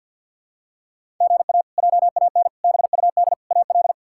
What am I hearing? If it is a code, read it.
Morse code, heard as a keyed tone, about 47 wpm, 700 Hz. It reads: GA JAN BUD AL
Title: K1USN SST QSO Practice - 50wpm